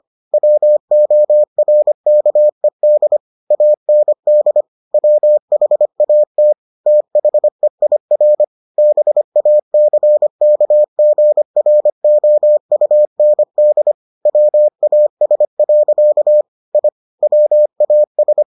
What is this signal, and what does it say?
Morse code, heard as a keyed tone, about 25 wpm, 600 Hz. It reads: WORKED AND WHAT THEIR BACKGROUND WAS. I WAS